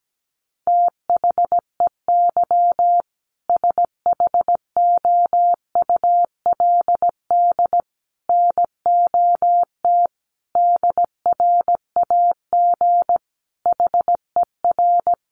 Morse code, heard as a keyed tone, about 17 wpm, 700 Hz. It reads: THEY SHOULD NOT DRAG HER
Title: Sentences from Top 1500 Words 17wpm